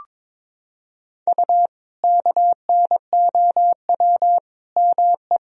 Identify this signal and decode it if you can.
Morse code, heard as a keyed tone, about 22 wpm, 700 Hz. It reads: U KNOW ME
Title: Sentences from Top 400 Words and Abbreviations 22wpm